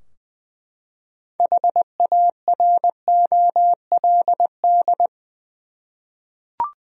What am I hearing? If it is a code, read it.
Morse code, heard as a keyed tone, about 20 wpm, 700 Hz. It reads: HAROLD
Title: USA Common Names - 20wpm